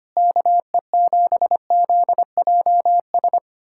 Morse code, heard as a keyed tone, about 25 wpm, 700 Hz. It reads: KE7ZJS
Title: USA Callsign DE Callsign - 6 Characters - 25wpm